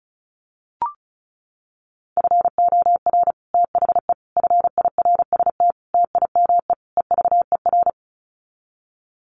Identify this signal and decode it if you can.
Morse code, heard as a keyed tone, about 34 wpm, 700 Hz. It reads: FOR THE FIRST TIME EVER
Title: Sets of 5 Words Encore2 35wpm